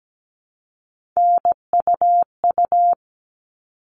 Morse code, heard as a keyed tone, about 17 wpm, 700 Hz. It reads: NUU
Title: All Cut Numbers - 3-Digit 17wpm